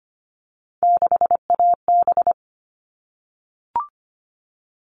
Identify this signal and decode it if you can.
Morse code, heard as a keyed tone, about 25 wpm, 700 Hz. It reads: 6AB
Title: All Cut Numbers - 3-Digit 25wpm